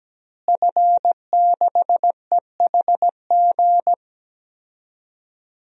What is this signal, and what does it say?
Morse code, heard as a keyed tone, about 17 wpm, 700 Hz. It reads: F6EHG
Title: DXCC-100 Call Signs - 5 Characters - Repeated 3x - 17wpm